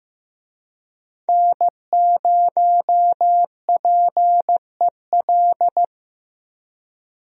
Morse code, heard as a keyed tone, about 15 wpm, 700 Hz. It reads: N0PEL
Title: USA Call Signs - 5 Characters - Repeated 3x - 15wpm